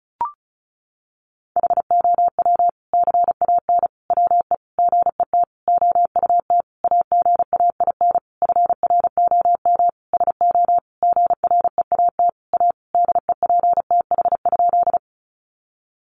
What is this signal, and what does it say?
Morse code, heard as a keyed tone, about 34 wpm, 700 Hz. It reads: HOW CAN WE GET OUT AGAIN FROM SO GREAT A DEPTH?
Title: Sentences from Top 1200 Words 35wpm